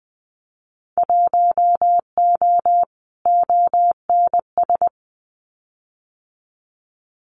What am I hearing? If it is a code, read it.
Morse code, heard as a keyed tone, about 20 wpm, 700 Hz. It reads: 1O ONS
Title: Winter Field Day Exchanges Encore 20wpm